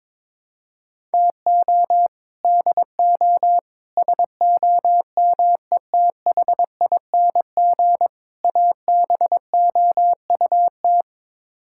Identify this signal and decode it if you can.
Morse code, heard as a keyed tone, about 22 wpm, 700 Hz. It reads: TO DO SOMETHING ABOUT
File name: practice / 22wpm / Build A Phrase - 5-Word Phrases - 22wpm